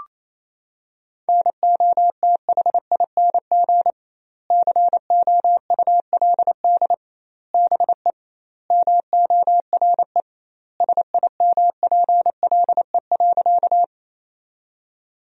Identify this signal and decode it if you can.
Morse code, heard as a keyed tone, about 28 wpm, 700 Hz. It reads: NOTHING COULD BE MORE SIMPLE.
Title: 2x Word-Spacing - Sentences from Top 500 Words - 28wpm